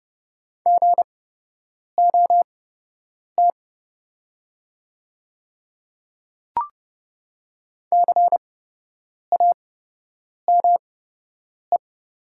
Morse code, heard as a keyed tone, about 30 wpm, spaced as for 8 wpm, 700 Hz. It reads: GOT CAME
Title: 8x Character Spacing - Top 300 Words - 30wpm